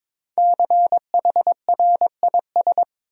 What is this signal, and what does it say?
Morse code, heard as a keyed tone, about 22 wpm, 700 Hz. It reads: CHRIS
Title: USA Common Names - 22wpm